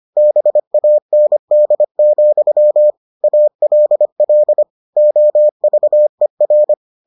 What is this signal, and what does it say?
Morse code, heard as a keyed tone, about 25 wpm, 600 Hz. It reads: BAND, ALL OVER